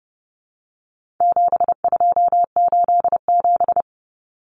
Morse code, heard as a keyed tone, about 30 wpm, 700 Hz. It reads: 7287
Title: Common Cut Numbers - 4-Digit 30wpm